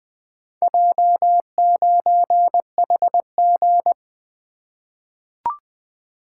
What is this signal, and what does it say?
Morse code, heard as a keyed tone, about 20 wpm, 700 Hz. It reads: J9HG
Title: Four Letter-Number Combinations - 20wpm